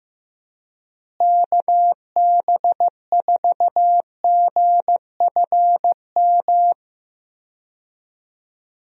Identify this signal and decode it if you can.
Morse code, heard as a keyed tone, about 15 wpm, 700 Hz. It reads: KB4GFM
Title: USA Call Signs - 6 Characters - Repeated 2x - 15wpm